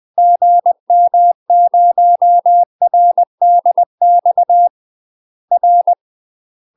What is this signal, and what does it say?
Morse code, heard as a keyed tone, about 20 wpm, 700 Hz. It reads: GM0RDX R
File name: UK MASTER QSOs TRAINING / 20WPM / UK MASTER QSOs 0017